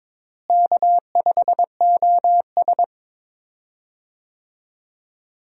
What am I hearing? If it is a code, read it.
Morse code, heard as a keyed tone, about 22 wpm, 700 Hz. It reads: K5OS
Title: USA Call Signs - 4 Characters - Repeated 3x - 22wpm